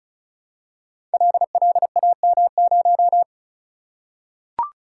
Morse code, heard as a keyed tone, about 36 wpm, 700 Hz. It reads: LLAM0
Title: Code Groups - 5 Letter:Numbers - 35wpm